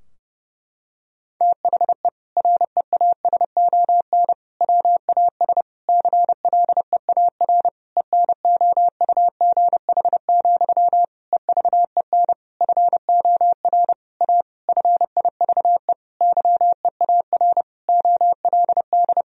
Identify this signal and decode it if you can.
Morse code, heard as a keyed tone, about 32 wpm, 700 Hz. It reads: THE REASON WAS CLEAR ENOUGH, EVEN FOR A FIVE YEAR OLD
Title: Sentences from Top 1000 Words - Review Part 2 - 30wpm